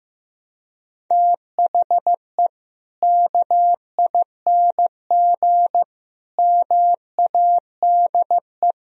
Morse code, heard as a keyed tone, about 15 wpm, 700 Hz. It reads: THE KING MADE
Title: Sentences from Top 1000 Words - Review Part 4 - 15wpm